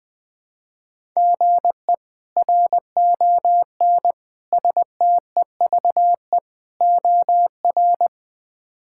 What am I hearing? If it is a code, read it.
Morse code, heard as a keyed tone, about 20 wpm, 700 Hz. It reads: GE RON STEVE OR
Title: K1USN SST QSO Practice - 20wpm